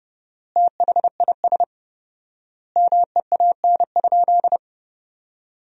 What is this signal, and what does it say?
Morse code, heard as a keyed tone, about 30 wpm, 700 Hz. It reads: THIS MEAN?
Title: 4x Word-Spacing - Sentences from Top 200 Words - 30wpm